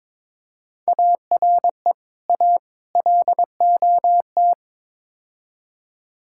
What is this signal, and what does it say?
Morse code, heard as a keyed tone, about 23 wpm, 700 Hz. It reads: ARE A LOT
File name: practice / 22wpm / Build A Phrase - 5-Word Phrases - 22wpm